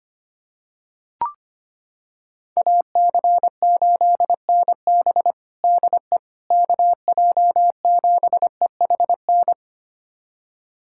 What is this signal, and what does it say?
Morse code, heard as a keyed tone, about 25 wpm, 700 Hz. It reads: AC8NB DE KJ7EHN
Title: USA Callsign DE Callsign 25wpm